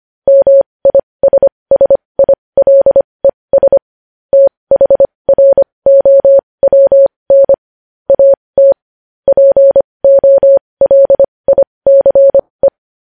Morse code, heard as a keyed tone, about 25 wpm, spaced as for 20 wpm, 550 Hz. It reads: MISSILES THROWN AT POLICE
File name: podcast / News headlines-1x-25x20